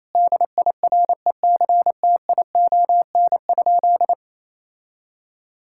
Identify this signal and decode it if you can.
Morse code, heard as a keyed tone, about 28 wpm, 700 Hz. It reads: DIRECTION?
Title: Sentences from Top 1000 Words - Review Part 5 - 28wpm